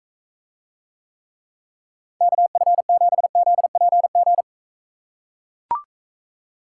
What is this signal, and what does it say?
Morse code, heard as a keyed tone, about 44 wpm, 700 Hz. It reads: KF7ZPG